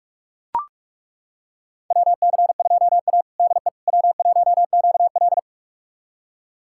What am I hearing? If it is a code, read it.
Morse code, heard as a keyed tone, about 41 wpm, 700 Hz. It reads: WC2A DE W1QL
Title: USA Callsign DE Callsign - 4 Characters - 45wpm